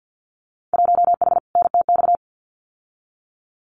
Morse code, heard as a keyed tone, about 50 wpm, 700 Hz. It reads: 1H NTX
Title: Winter Field Day Exchanges 50wpm